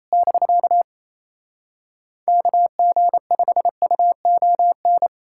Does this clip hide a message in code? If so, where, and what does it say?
Yes: Morse code from 0.1 to 5.1 s, heard as a keyed tone, about 37 wpm, 700 Hz. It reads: <BK> KG5UON